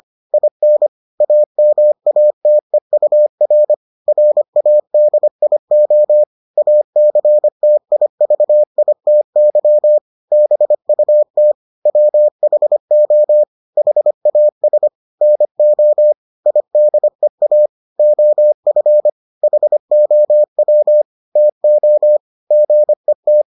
Morse code, heard as a keyed tone, about 25 wpm, 600 Hz. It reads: IN AMATEUR RADIO ACTIVITY BUT WHO HAS NO IDEA OF HOW TO GET